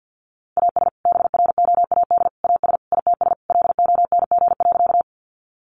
Morse code, heard as a keyed tone, about 50 wpm, 700 Hz. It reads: AS BROAD AS ITS LONG.
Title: Sentences from Top 1000 Words 50wpm